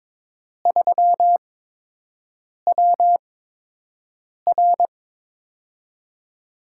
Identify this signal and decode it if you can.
Morse code, heard as a keyed tone, about 22 wpm, spaced as for 6 wpm, 700 Hz. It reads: 3WR
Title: USA Call Signs - 5 Characters - 8x Character Spacing - 22wpm